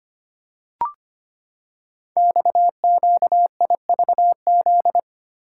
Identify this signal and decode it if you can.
Morse code, heard as a keyed tone, about 25 wpm, 700 Hz. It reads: XQIVZ
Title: Code Groups - 5 Letters - 25wpm